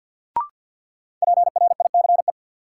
Morse code, heard as a keyed tone, about 51 wpm, 700 Hz. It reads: PRICE
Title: Sentences from Top 600 Words - New Word Drill - 50wpm